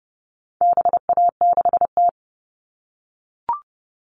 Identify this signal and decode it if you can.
Morse code, heard as a keyed tone, about 30 wpm, 700 Hz. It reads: BA6T
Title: All Cut Numbers - 4-Digit 30wpm